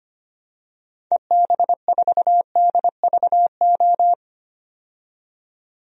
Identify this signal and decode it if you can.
Morse code, heard as a keyed tone, about 25 wpm, 700 Hz. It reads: EB4DVO
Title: DXCC-100 Call Signs - 6 Characters - 25wpm